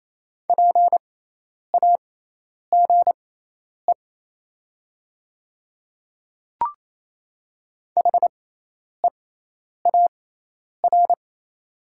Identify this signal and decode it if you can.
Morse code, heard as a keyed tone, about 28 wpm, spaced as for 10 wpm, 700 Hz. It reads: PAGE HEAR
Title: 6x Character Spacing - Top 300 Words - 28wpm